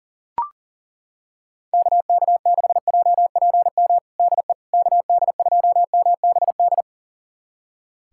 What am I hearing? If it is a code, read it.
Morse code, heard as a keyed tone, about 40 wpm, 690 Hz. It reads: KK6JPM DE KD2MBD
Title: USA Callsign DE Callsign - 6 Characters - 40wpm